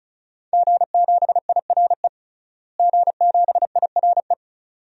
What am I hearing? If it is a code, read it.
Morse code, heard as a keyed tone, about 36 wpm, 700 Hz. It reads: G7IRE G7IRE
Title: DXCC-100 Call Signs - 5 Characters - Repeated 2x - 35wpm